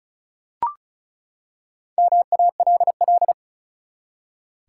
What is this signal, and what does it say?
Morse code, heard as a keyed tone, about 36 wpm, 700 Hz. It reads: MALL
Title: Four Letter Words - 35wpm